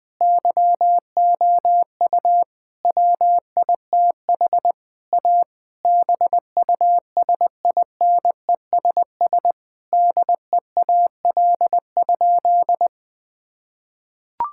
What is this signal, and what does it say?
Morse code, heard as a keyed tone, about 19 wpm, 700 Hz. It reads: YOU WITH A BUSINESS DEAL?
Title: Sentences from Top 1000 Words - Review Part 6 - 20wpm